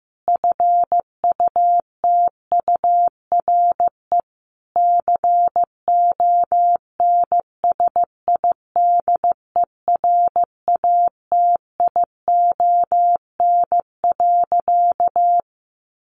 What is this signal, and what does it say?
Morse code, heard as a keyed tone, about 14 wpm, 700 Hz. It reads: FUTURE CONSIDERATION.
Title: Sentences from Top 1500 Words 15wpm